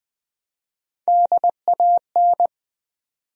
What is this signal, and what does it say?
Morse code, heard as a keyed tone, about 21 wpm, 700 Hz. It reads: DAN
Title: USA Common Names - Max 4 Letters - 20wpm